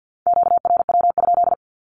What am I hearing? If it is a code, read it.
Morse code, heard as a keyed tone, about 50 wpm, 700 Hz. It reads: QRW?
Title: Q-Codes Repeated 3x 50wpm